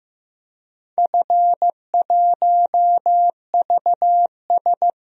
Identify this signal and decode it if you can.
Morse code, heard as a keyed tone, about 15 wpm, 700 Hz. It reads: F1VS
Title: DXCC-100 Call Signs - 4 Characters - Repeated 2x - 15wpm